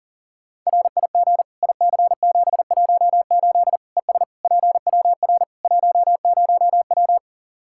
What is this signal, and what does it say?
Morse code, heard as a keyed tone, about 40 wpm, 700 Hz. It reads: RIG IC718 ES PWR 10W
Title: Practice QSO - Exchange 2 - Info - 40wpm